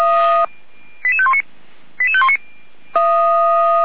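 Five-tone selective call (ZVEI), heard as quick runs of five tones, tones of about 70 ms. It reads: 88519 88519
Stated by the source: ZVEI five-tone call 88119 received on an SDR.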